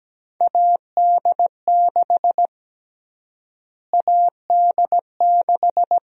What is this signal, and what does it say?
Morse code, heard as a keyed tone, about 17 wpm, 700 Hz. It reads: AD6 AD6
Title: USA Call Sign Prefixes - Repeated 2x - 17wpm